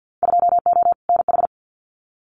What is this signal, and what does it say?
Morse code, heard as a keyed tone, about 50 wpm, 700 Hz. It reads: 2O NH
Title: Winter Field Day Exchanges 50wpm